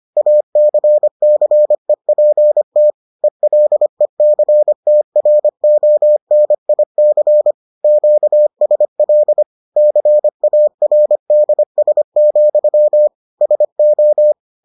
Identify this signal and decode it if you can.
Morse code, heard as a keyed tone, about 25 wpm, 600 Hz. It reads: ACCEPT ELECTRONIC QSL CARDS, SO